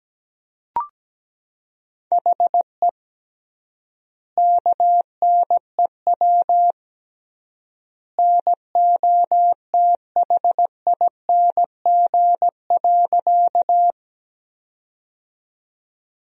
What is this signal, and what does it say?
Morse code, heard as a keyed tone, about 17 wpm, 700 Hz. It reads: HE KNEW NOTHING.
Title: 3x Word-Spacing - Sentences from Top 500 Words - 17wpm